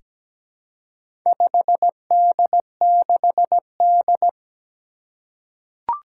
Morse code, heard as a keyed tone, about 17 wpm, 700 Hz. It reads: 5D6D
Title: Four Challenging Letter-Number Combinations 17wpm